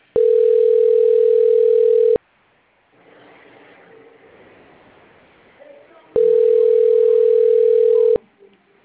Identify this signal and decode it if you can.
Ringback tone (North America), 440 + 480 Hz, with speech.
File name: rg-1600-unknown- - - .65229